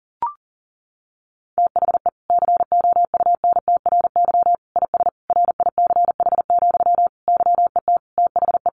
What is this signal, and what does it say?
Morse code, heard as a keyed tone, about 40 wpm, 700 Hz. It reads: THE COUNTRY IS RICH, YET THE
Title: Sentences from Top 1100 Words 40wpm